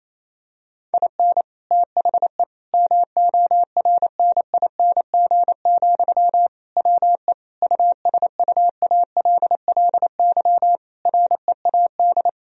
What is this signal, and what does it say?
Morse code, heard as a keyed tone, about 28 wpm, 700 Hz. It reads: IN THE MORNING, WE USUALLY READ
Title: Sentences from Top 1000 Words - Review Part 3 - 28wpm